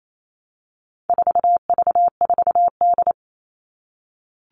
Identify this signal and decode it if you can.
Morse code, heard as a keyed tone, about 28 wpm, 700 Hz. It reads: 4V4D
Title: All Cut Numbers - 4-Digit 28wpm